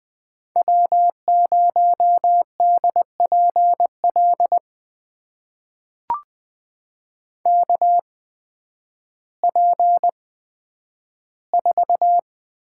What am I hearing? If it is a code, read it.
Morse code, heard as a keyed tone, about 20 wpm, spaced as for 11 wpm, 700 Hz. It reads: W0DPL KP4
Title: USA Call Signs - 5 Characters - 8x Character Spacing - 20wpm